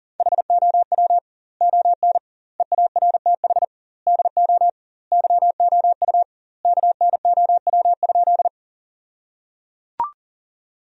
Morse code, heard as a keyed tone, about 37 wpm, 700 Hz. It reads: HOW ON EARTH DO YOU KNOW?
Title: 2x Word-Spacing - Sentences from Top 200 Words - 40wpm